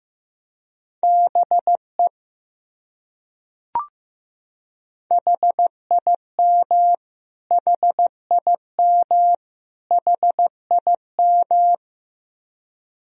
Morse code, heard as a keyed tone, about 15 wpm, 700 Hz. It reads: BE HIM HIM HIM
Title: Top 100 Words Repeated 15wpm